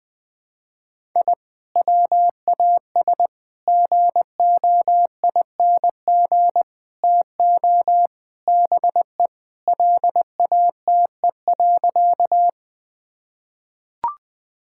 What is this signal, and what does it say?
Morse code, heard as a keyed tone, about 20 wpm, 700 Hz. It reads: I WAS GOING TO BE LATE.
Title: Sentences with 5-Word Phrases - 20wpm